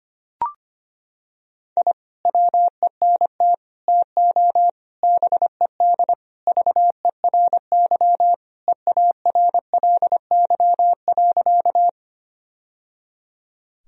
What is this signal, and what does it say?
Morse code, heard as a keyed tone, about 25 wpm, 700 Hz. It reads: I WENT TO BED VERY EARLY.
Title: Sentences from Top 1000 Words - Review Part 2 - 25wpm